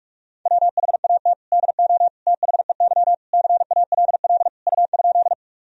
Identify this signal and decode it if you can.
Morse code, heard as a keyed tone, about 46 wpm, 700 Hz. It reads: WHAT DO THEY CALL U?
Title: Sentences from Top 100 Words and Abbreviations 45wpm